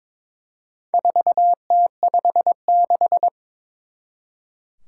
Morse code, heard as a keyed tone, about 22 wpm, 700 Hz. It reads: 4T56
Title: Four Challenging Letter-Number Combinations 22wpm